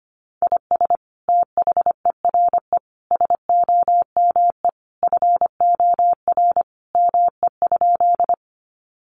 Morse code, heard as a keyed tone, about 25 wpm, 700 Hz. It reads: IS THERE SOME FOR ME?
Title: Sentences from Top 200 Words 25wpm